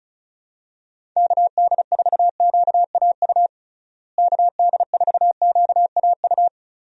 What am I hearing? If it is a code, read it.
Morse code, heard as a keyed tone, about 34 wpm, 700 Hz. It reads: KD4QAU KD4QAU
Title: USA Call Signs - 6 Characters - Repeated 3x - 35wpm